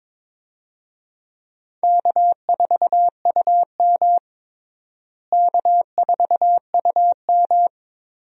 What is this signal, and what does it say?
Morse code, heard as a keyed tone, about 22 wpm, 700 Hz. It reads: K4UM K4UM
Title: USA Call Signs - 4 Characters - Repeated 3x - 22wpm